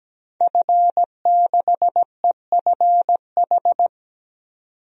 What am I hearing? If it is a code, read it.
Morse code, heard as a keyed tone, about 17 wpm, 700 Hz. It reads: F6EFH
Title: DXCC-100 Call Signs - 5 Characters - 17wpm